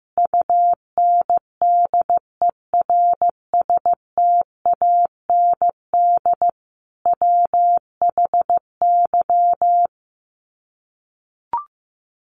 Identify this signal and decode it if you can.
Morse code, heard as a keyed tone, about 15 wpm, 700 Hz. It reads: UNDERSTAND WHY
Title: Sets of 5 Words Encore 15wpm